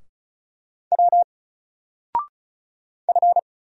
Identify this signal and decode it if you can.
Morse code, heard as a keyed tone, about 36 wpm, 700 Hz. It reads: W F
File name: Single-Letter-Number-35wpm